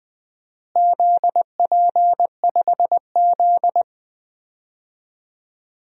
Morse code, heard as a keyed tone, about 20 wpm, 700 Hz. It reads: ZP5Z
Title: DXCC-100 Call Signs - 4 Characters - Repeated 3x - 20wpm